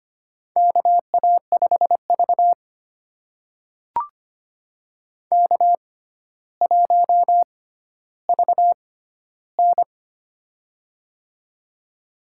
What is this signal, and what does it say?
Morse code, heard as a keyed tone, about 25 wpm, spaced as for 13 wpm, 700 Hz. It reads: KA5V K1VN
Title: USA Call Signs - 4 Characters - 6x Character Spacing - 25wpm